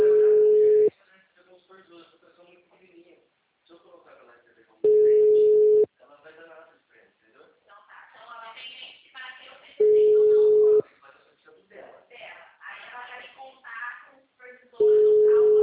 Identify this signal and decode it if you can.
Ringback tone (Europe, 425 Hz), with speech.